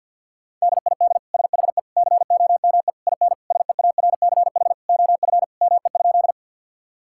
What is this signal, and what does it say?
Morse code, heard as a keyed tone, about 50 wpm, 700 Hz. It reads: DID SHE COME IN SEARCH OF ME?
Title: Sentences from Top 1000 Words - Review Part 5 - 50wpm